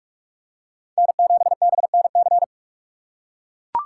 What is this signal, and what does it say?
Morse code, heard as a keyed tone, about 43 wpm, 700 Hz. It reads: N7BNC